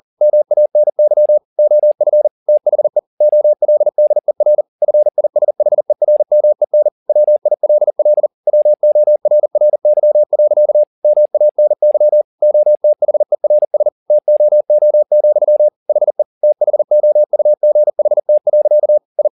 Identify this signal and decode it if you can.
Morse code, heard as a keyed tone, about 40 wpm, 600 Hz. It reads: MANY OF THE OLDER FISHERMEN WILL WORRY. MANY OTHERS TOO, HE THOUGHT. I
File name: the old man and the sea 2 40wpm